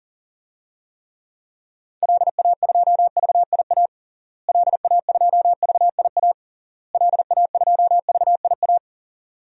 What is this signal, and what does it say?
Morse code, heard as a keyed tone, about 37 wpm, 700 Hz. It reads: LA2VIA LA2VIA LA2VIA
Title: DXCC-100 Call Signs - 6 Characters - Repeated 3x - 40wpm